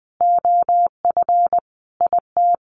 Morse code, heard as a keyed tone, about 20 wpm, 700 Hz. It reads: OF IT
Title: Sentences from Top 1200 Words 20wpm